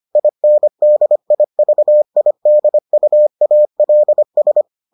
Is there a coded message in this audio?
Yes: Morse code from 0.2 to 4.6 s, heard as a keyed tone, about 25 wpm, 600 Hz. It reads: INDIVIDUALS